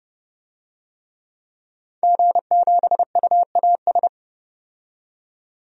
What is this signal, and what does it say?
Morse code, heard as a keyed tone, about 30 wpm, 700 Hz. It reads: G7UAS
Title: DXCC-100 Call Signs - 5 Characters - 30wpm